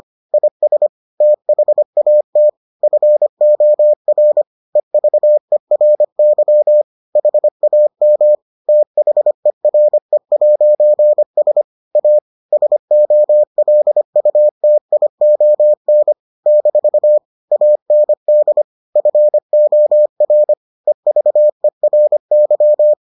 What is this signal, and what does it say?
Morse code, heard as a keyed tone, about 25 wpm, 600 Hz. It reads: IS THAT FOR EVERY HAM THERE'S A SOLUTION - AND FOR EVERY